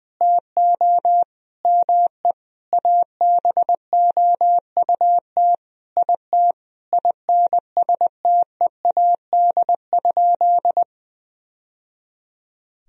Morse code, heard as a keyed tone, about 20 wpm, 700 Hz. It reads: TO ME ABOUT IT INSTEAD?
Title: Sentences from Top 1000 Words - Review Part 4 - 20wpm